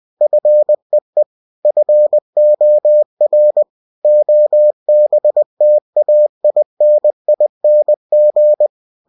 Morse code, heard as a keyed tone, about 20 wpm, 600 Hz. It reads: FEE FOR OBTAINING
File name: cw-books / winnie-the-pooh / winnie-20wpm-page-0113